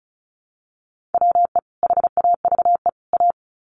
Morse code, heard as a keyed tone, about 34 wpm, 700 Hz. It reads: WE HAVE A
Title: Sets of 3 Words 35wpm